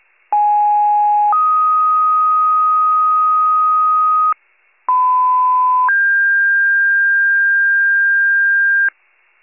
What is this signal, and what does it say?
Two-tone page, heard as a one-second tone, then a three-second tone, about 830 then 1250 Hz, about 1010 then 1600 Hz, with noise. It is heard from 0.3 to 8.9 s.
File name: pager / 94 RESCUE DUTY / wcalert